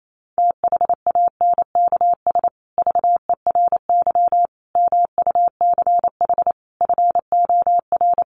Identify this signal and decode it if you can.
Morse code, heard as a keyed tone, about 28 wpm, 700 Hz. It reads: THANKS VERY MUCH FOR
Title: Sets of 5 Words Encore2 28wpm